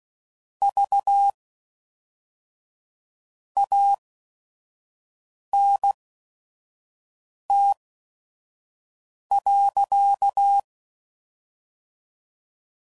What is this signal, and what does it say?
Morse code, heard as a keyed tone, about 16 wpm, spaced as for 5 wpm, 800 Hz. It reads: V ANT.
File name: CW Code Groups @5WPM / QSOs / WD0FBT 2